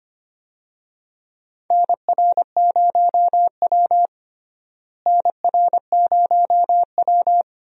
Morse code, heard as a keyed tone, about 25 wpm, 700 Hz. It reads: NR0W NR0W
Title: USA Call Signs - 4 Characters - Repeated 2x - 25wpm